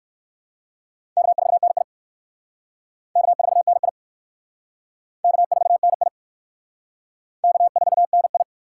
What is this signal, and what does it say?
Morse code, heard as a keyed tone, about 56 wpm, 700 Hz. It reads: K4NI K4NI K4NI K4NI
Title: Speed Racing - USA 1x2 Call Signs 45wpm